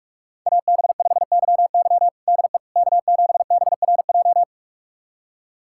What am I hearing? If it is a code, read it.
Morse code, heard as a keyed tone, about 46 wpm, 700 Hz. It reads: AB5YY DE K7BRJ
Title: USA Callsign DE Callsign - 5 Characters - 45wpm